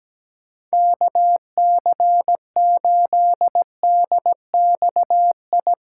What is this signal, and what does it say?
Morse code, heard as a keyed tone, about 18 wpm, 700 Hz. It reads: KC8DXI
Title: Speed Racing - USA 2x3 Call Signs 17wpm